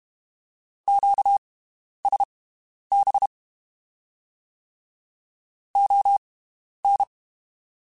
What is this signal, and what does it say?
Morse code, heard as a keyed tone, about 32 wpm, spaced as for 11 wpm, 800 Hz. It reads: QSB ON
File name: (spaced) / QSOs Spaced-mp3s / WD0BGP 1 @30 sp